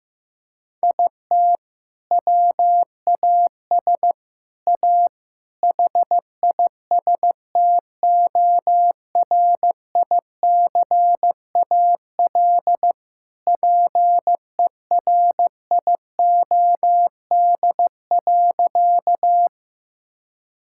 Morse code, heard as a keyed tone, about 15 wpm, 700 Hz. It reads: IT WAS A HISTORICAL PERIOD.
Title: Sentences from Top 1000 Words - Review Part 5 - 15wpm